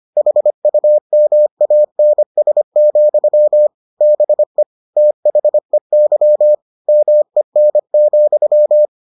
Morse code, heard as a keyed tone, about 25 wpm, 600 Hz. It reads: HUMANS, BE THEY MEN,